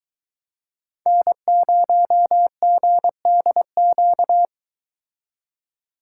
Morse code, heard as a keyed tone, about 23 wpm, 700 Hz. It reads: N0GDQ